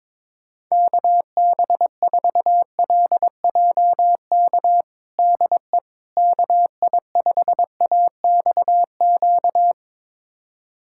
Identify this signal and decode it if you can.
Morse code, heard as a keyed tone, about 22 wpm, 700 Hz. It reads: KB4LJK DE KI5AXQ
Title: USA Callsign DE Callsign - 6 Characters - 22wpm